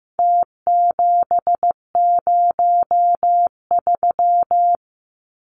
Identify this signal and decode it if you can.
Morse code, heard as a keyed tone, about 15 wpm, 700 Hz. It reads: T703
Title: Numbers - Contest Style - 4-Digit 15wpm